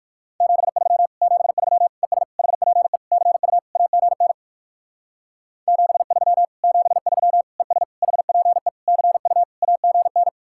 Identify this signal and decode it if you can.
Morse code, heard as a keyed tone, about 60 wpm, 700 Hz. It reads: 73 73 ES HPE CU AGN 73 73 ES HPE CU AGN